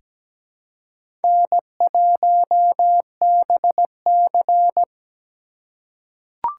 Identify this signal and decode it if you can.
Morse code, heard as a keyed tone, about 17 wpm, 700 Hz. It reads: N1BC
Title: USA Call Signs - 4 Characters - 4x Character Spacing - 17wpm